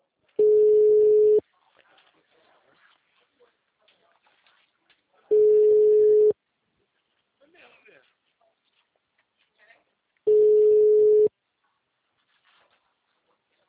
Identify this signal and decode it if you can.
Ringback tone (Europe, 425 Hz), with speech.